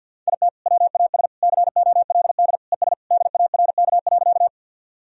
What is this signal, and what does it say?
Morse code, heard as a keyed tone, about 49 wpm, 700 Hz. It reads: IT WAS COLD ES DARK.
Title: Sentences from Top 500 Words and Abbreviations 50wpm